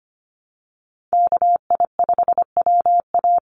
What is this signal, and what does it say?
Morse code, heard as a keyed tone, about 25 wpm, 700 Hz. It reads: KI5WA
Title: NA SOTA Chaser Callsigns 25wpm